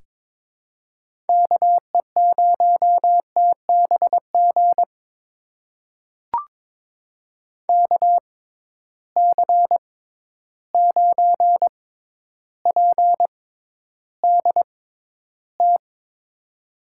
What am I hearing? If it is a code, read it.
Morse code, heard as a keyed tone, about 23 wpm, spaced as for 11 wpm, 700 Hz. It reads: KE0TBG KC9PDT
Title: USA Call Signs - 6 Characters - 6x Character Spacing - 22wpm